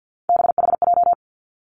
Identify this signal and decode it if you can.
Morse code, heard as a keyed tone, about 50 wpm, 700 Hz. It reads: BHJ
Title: Three Challenging Letter-Number Combinations 50wpm